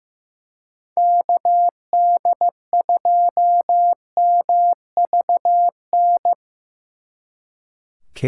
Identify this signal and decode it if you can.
Morse code, heard as a keyed tone, about 17 wpm, 700 Hz. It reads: KD2MVN
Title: Speed Racing - USA 2x3 Call Signs 15wpm